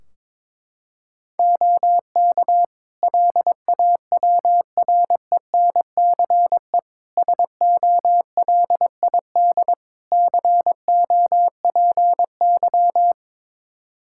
Morse code, heard as a keyed tone, about 22 wpm, 700 Hz. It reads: OK LAWRENCE SOLID COPY
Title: Practice QSO - Exchange 3 - More Info - 22wpm